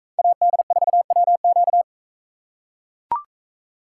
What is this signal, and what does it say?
Morse code, heard as a keyed tone, about 42 wpm, 700 Hz. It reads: AD4WQ